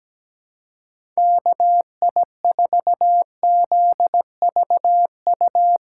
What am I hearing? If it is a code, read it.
Morse code, heard as a keyed tone, about 17 wpm, 700 Hz. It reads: KI4ZVU
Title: K1USN SST QSO Practice - 17wpm